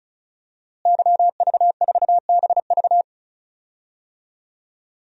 Morse code, heard as a keyed tone, about 36 wpm, 700 Hz. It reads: YV4BV
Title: DXCC-100 Call Signs - 5 Characters - Repeated 3x - 35wpm